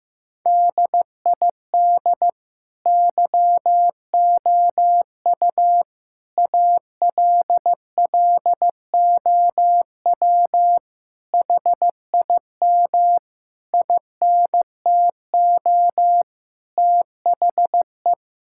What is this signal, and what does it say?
Morse code, heard as a keyed tone, about 16 wpm, 700 Hz. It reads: DID YOU ALLOW HIM INTO THE
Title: Sentences from Top 1000 Words - Review Part 5 - 15wpm